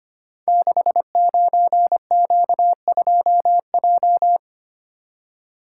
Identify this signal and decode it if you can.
Morse code, heard as a keyed tone, about 25 wpm, 700 Hz. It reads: 69Q2J
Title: Code Groups - 5 Letter:Numbers - 25wpm